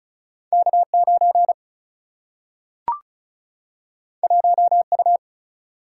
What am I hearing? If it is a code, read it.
Morse code, heard as a keyed tone, about 33 wpm, 700 Hz. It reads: K9 1U
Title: Single-Letter Single-Number 35wpm